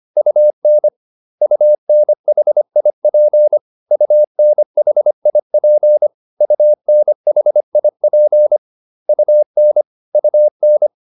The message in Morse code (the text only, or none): UN UNHIP UNHIP UNHIP UN UN